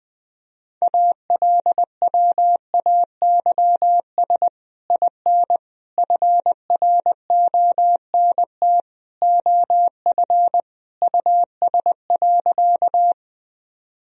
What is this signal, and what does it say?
Morse code, heard as a keyed tone, about 20 wpm, 700 Hz. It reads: ALWAYS IN FRONT OF US.
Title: Sentences from Top 1000 Words - Review Part 2 - 20wpm